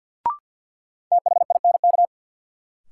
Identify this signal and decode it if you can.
Morse code, heard as a keyed tone, about 53 wpm, 700 Hz. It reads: THINK
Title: Sets of 2 Words - New Word Drill - 50wpm